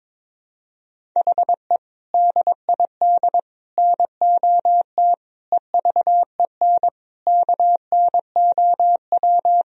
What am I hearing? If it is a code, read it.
Morse code, heard as a keyed tone, about 22 wpm, 700 Hz. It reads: HE DID NOT EVEN KNOW
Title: Sentences from Top 1000 Words - Review Part 1 - 22wpm